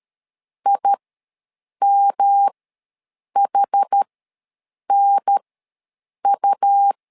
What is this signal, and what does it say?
Morse code, heard as a keyed tone, about 13 wpm, spaced as for 7 wpm, 800 Hz. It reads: IMHNU
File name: LEZIONI / 3 MWRO / 3 M ws 12wpm 1745